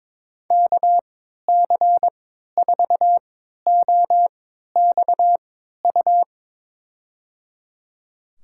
Morse code, heard as a keyed tone, about 22 wpm, spaced as for 13 wpm, 700 Hz. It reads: KC4OXU
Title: USA Call Signs - 6 Characters - 3x Character Spacing - 22wpm